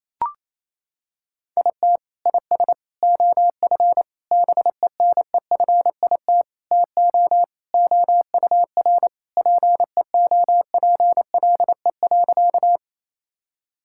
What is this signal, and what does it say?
Morse code, heard as a keyed tone, about 28 wpm, 700 Hz. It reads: IT IS OF BENEFIT TO OUR PEOPLE.
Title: Sentences from Top 1000 Words - Review Part 6 - 28wpm